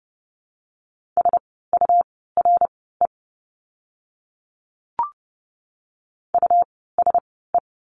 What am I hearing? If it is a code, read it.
Morse code, heard as a keyed tone, about 30 wpm, spaced as for 17 wpm, 700 Hz. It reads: SURE USE
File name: Lesson 17f - Letter u - Words 30wpm